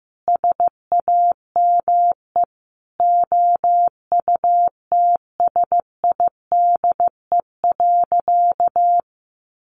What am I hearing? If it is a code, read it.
Morse code, heard as a keyed tone, about 15 wpm, 700 Hz. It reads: SAME OUTSIDE.
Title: Sentences from Top 900 Words 15wpm